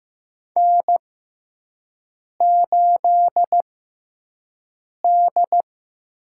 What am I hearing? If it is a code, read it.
Morse code, heard as a keyed tone, about 15 wpm, spaced as for 5 wpm, 700 Hz. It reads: N8D
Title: USA Call Signs - 6x Character Spacing - 15wpm